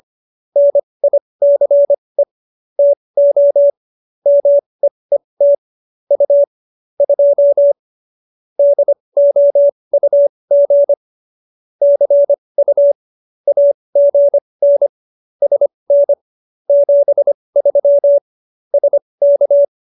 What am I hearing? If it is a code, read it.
Morse code, heard as a keyed tone, about 25 wpm, spaced as for 20 wpm, 570 Hz. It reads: NICE TO MEET U 2 DOUG CU AGN SN 73 SK
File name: qso220b 20